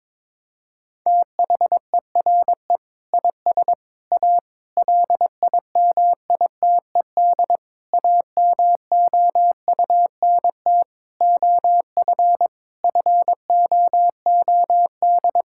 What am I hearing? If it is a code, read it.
Morse code, heard as a keyed tone, about 21 wpm, 700 Hz. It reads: THERE IS A LIMITED AMOUNT OF FOOD
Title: Sentences from Top 1000 Words - Review Part 6 - 22wpm